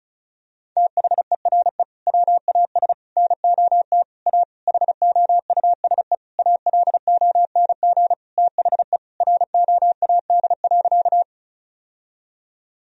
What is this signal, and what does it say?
Morse code, heard as a keyed tone, about 34 wpm, 700 Hz. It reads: THERE WAS NOT A HOUSE ALONG THE ROAD.
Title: Sentences from Top 1000 Words - Review Part 2 - 35wpm